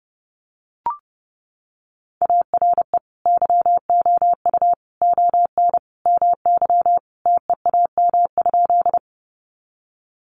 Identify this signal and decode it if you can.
Morse code, heard as a keyed tone, about 30 wpm, 700 Hz. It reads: ARE YOU ON MY TEAM?
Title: Sentences from Top 1500 Words 30wpm